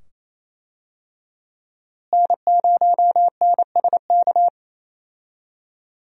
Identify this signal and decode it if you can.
Morse code, heard as a keyed tone, about 28 wpm, 700 Hz. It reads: N0NSK
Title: USA Call Signs - 5 Characters - Repeated 2x - 28wpm